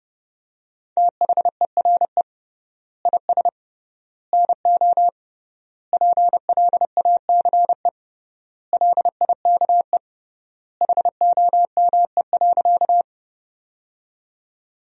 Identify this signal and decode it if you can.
Morse code, heard as a keyed tone, about 29 wpm, 700 Hz. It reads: THERE IS NO PLACE LIKE HOME.
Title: 3x Word-Spacing - Sentences from Top 200 Words - 30wpm